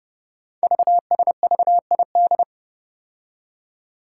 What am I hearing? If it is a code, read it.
Morse code, heard as a keyed tone, about 30 wpm, 700 Hz. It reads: VSVID
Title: Code Groups - 5 Letters - 30wpm